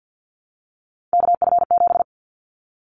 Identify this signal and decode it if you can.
Morse code, heard as a keyed tone, about 50 wpm, 700 Hz. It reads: KF7
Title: Callsign Prefixes for Top 25 DXCC - Repeated 3x - 50wpm